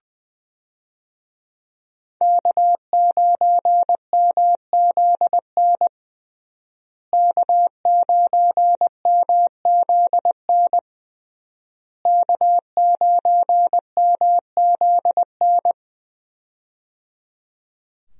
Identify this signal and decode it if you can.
Morse code, heard as a keyed tone, about 20 wpm, 700 Hz. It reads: K9MZN K9MZN K9MZN
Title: USA Call Signs - 5 Characters - Repeated 3x - 20wpm